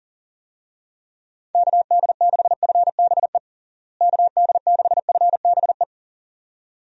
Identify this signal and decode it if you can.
Morse code, heard as a keyed tone, about 37 wpm, 700 Hz. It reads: KD6FBE KD6FBE
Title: USA Call Signs - 6 Characters - Repeated 2x - 40wpm